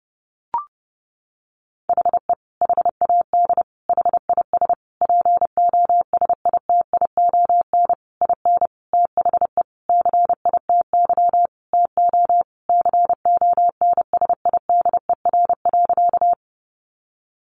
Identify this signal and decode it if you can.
Morse code, heard as a keyed tone, about 30 wpm, 700 Hz. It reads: HE HAD HIS POSITION IN THE CITY TO CONSIDER.
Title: Sentences from Top 900 Words 30wpm